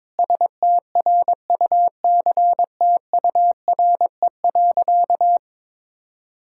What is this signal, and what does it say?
Morse code, heard as a keyed tone, about 22 wpm, 700 Hz. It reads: STRUCTURE.
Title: Sentences from Top 1000 Words - Review Part 6 - 22wpm